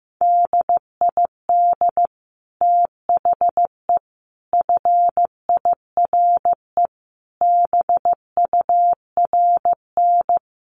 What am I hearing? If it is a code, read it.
Morse code, heard as a keyed tone, about 15 wpm, 700 Hz. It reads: DID THE FIRE BURN
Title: Sentences from Top 1300 Words 15wpm